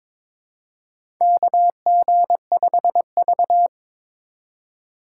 Morse code, heard as a keyed tone, about 22 wpm, 700 Hz. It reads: KG5V
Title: K1USN SST QSO Practice - 22wpm